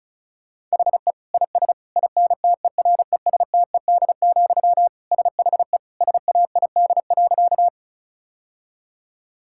Morse code, heard as a keyed tone, about 36 wpm, 700 Hz. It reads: HE IS INTERESTED, SHE SAID.
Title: Sentences from Top 1000 Words - Review Part 6 - 35wpm